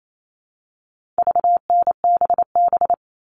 Morse code, heard as a keyed tone, about 28 wpm, 700 Hz. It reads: VNBB
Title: All Cut Numbers - 4-Digit 28wpm